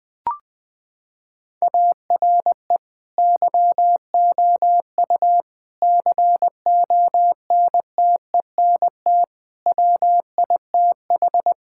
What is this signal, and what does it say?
Morse code, heard as a keyed tone, about 20 wpm, 700 Hz. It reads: ARE YOU CONTENT WITH
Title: Sentences from Top 1000 Words - Review Part 6 - 20wpm